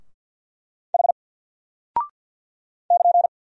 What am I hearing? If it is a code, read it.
Morse code, heard as a keyed tone, about 50 wpm, 700 Hz. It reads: H <KN>
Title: Single Character (Letter:Number:Punctuation:Prosign) 50wpm